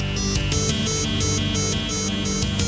There is no code.